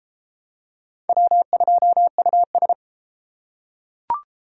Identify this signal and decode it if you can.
Morse code, heard as a keyed tone, about 30 wpm, 700 Hz. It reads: W2US